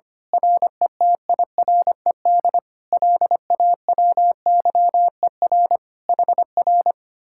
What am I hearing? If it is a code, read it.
Morse code, heard as a keyed tone, about 25 wpm, 700 Hz. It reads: RETIRED LAWYER HR
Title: Speed-Racing Common QSO Phrases - 25wpm